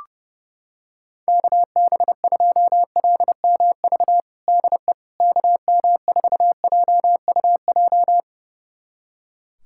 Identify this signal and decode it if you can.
Morse code, heard as a keyed tone, about 32 wpm, 700 Hz. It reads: KB2LMV DE KM4JUJ
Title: USA Callsign DE Callsign - 6 Characters - 30wpm